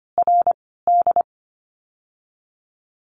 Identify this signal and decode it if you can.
Morse code, heard as a keyed tone, about 25 wpm, 700 Hz. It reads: R D
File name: course / 25:10wpm / 094 Lesson 16f - Letter c - Words 25wpm